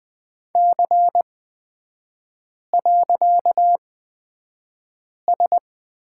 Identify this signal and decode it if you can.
Morse code, heard as a keyed tone, about 20 wpm, spaced as for 5 wpm, 700 Hz. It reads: C.S
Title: 255 Lesson 41e - Period - Three Characters 20wpm